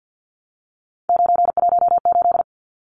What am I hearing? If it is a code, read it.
Morse code, heard as a keyed tone, about 50 wpm, 700 Hz. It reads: 918
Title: Numbers - 3-Digit 50wpm